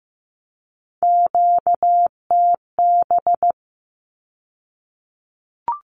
Morse code, heard as a keyed tone, about 15 wpm, 700 Hz. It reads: QTB
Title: Q-Codes with Spoken Meaning - 15wpm